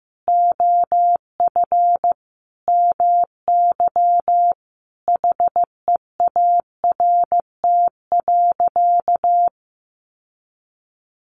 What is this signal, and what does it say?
Morse code, heard as a keyed tone, about 15 wpm, 700 Hz. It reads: OF MY HEART.
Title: Sentences from Top 1300 Words 15wpm